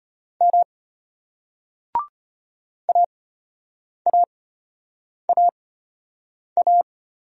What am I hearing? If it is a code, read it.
Morse code, heard as a keyed tone, about 31 wpm, spaced as for 8 wpm, 700 Hz. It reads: M AAAA
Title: Speed-Racing Single Letters 25wpm